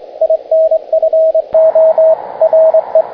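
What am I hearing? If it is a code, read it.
Morse code, heard as a keyed tone, about 23 wpm, 640 Hz, with radio noise. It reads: INFO RE
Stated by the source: Morse code demonstrating the SCAF-1 bandpass noise filter.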